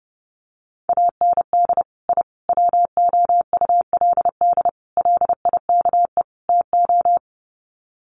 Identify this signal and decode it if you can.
Morse code, heard as a keyed tone, about 30 wpm, 700 Hz. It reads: AND I WOULD LIKE TO
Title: Sets of 5 Words Encore 30wpm